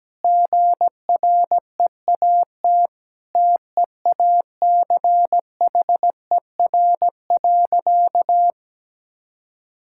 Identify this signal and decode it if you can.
Morse code, heard as a keyed tone, about 17 wpm, 700 Hz. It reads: GREAT TEACHER.
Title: Sentences from Top 1000 Words - Review Part 6 - 17wpm